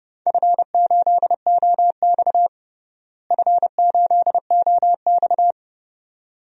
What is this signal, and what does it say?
Morse code, heard as a keyed tone, about 30 wpm, 700 Hz. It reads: F8OX F8OX
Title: DXCC-100 Call Signs - 4 Characters - Repeated 2x - 30wpm